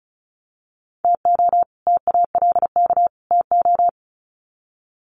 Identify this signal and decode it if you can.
Morse code, heard as a keyed tone, about 34 wpm, 700 Hz. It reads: TO TALK TO
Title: Sets of 3 Words 35wpm